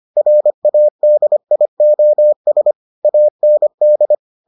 Morse code, heard as a keyed tone, about 25 wpm, 600 Hz. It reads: RADIOS AND